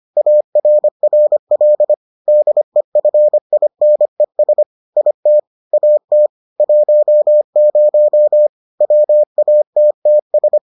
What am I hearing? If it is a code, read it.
Morse code, heard as a keyed tone, about 25 wpm, 600 Hz. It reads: ARRL DEFINES IT AT 10 WATTS